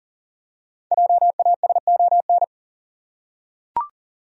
Morse code, heard as a keyed tone, about 40 wpm, 700 Hz. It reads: JASON
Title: USA Common Names - Max 5 Letters - 40wpm